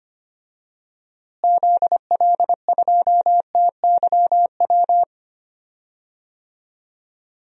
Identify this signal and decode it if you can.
Morse code, heard as a keyed tone, about 25 wpm, 700 Hz. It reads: ZL2TYW
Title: DXCC-100 Call Signs - 6 Characters - Repeated 2x - 25wpm